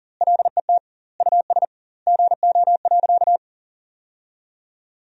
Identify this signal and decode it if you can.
Morse code, heard as a keyed tone, about 40 wpm, 700 Hz. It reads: LET US GO.
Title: 2x Word-Spacing - Sentences from Top 400 Words - 40wpm